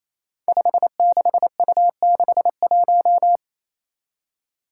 Morse code, heard as a keyed tone, about 28 wpm, 700 Hz. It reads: 56U61
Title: Five Challenging Letter-Number Combinations 28wpm